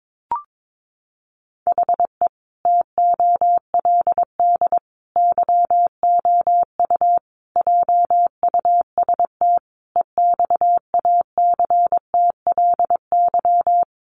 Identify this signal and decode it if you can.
Morse code, heard as a keyed tone, about 22 wpm, 700 Hz. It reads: HE TOLD YOU JUST EXACTLY
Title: Sentences from Top 900 Words 22wpm